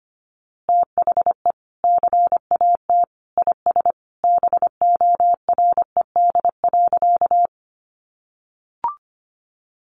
Morse code, heard as a keyed tone, about 25 wpm, 700 Hz. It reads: THE CAT IS BORED.
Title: Easy Phrases 25wpm